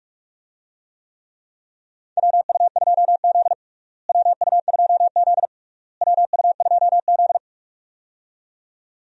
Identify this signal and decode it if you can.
Morse code, heard as a keyed tone, about 41 wpm, 700 Hz. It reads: WU2Z WU2Z WU2Z
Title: USA Call Signs - 4 Characters - Repeated 3x - 45wpm